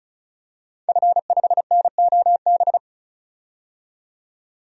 Morse code, heard as a keyed tone, about 36 wpm, 700 Hz. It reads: F5NOB
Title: DXCC-100 Call Signs - 5 Characters - 35wpm